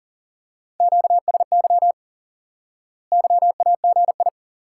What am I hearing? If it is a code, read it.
Morse code, heard as a keyed tone, about 40 wpm, 700 Hz. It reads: QSY YAGI
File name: IQE Course 3 - Lesson 89 - YAGI - Two Elements - 40wpm